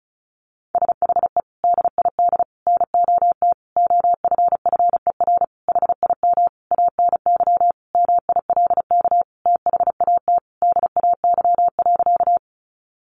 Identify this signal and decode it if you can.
Morse code, heard as a keyed tone, about 34 wpm, 700 Hz. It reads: SHE DID NOT OFFER HIM ANY MILK THAT DAY.
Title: Sentences from Top 1500 Words 35wpm